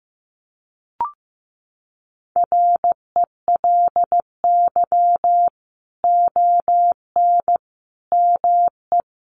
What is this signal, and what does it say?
Morse code, heard as a keyed tone, about 15 wpm, 700 Hz. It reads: RELY ON ME
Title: Sentences from Top 1200 Words 15wpm